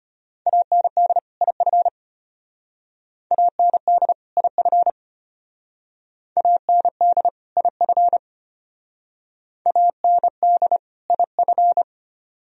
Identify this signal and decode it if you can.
Morse code, heard as a keyed tone, about 31 wpm, 700 Hz. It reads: AND IF AND IF AND IF AND IF
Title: Speed-Racing - Sets of 2 Words 25wpm